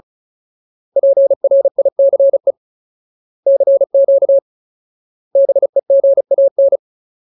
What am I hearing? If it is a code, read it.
Morse code, heard as a keyed tone, about 34 wpm, 550 Hz. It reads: PRICE CQ BEGAN
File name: Ultimate Word List / 35WPM Ultimate